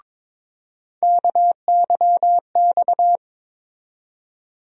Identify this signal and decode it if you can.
Morse code, heard as a keyed tone, about 21 wpm, 700 Hz. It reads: KYX
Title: USA Call Sign Postfixes - 3 Letters 22wpm